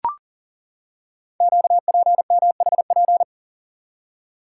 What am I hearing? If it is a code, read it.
Morse code, heard as a keyed tone, about 42 wpm, 690 Hz. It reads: QPMHP
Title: Code Groups - 5 Letters - 40wpm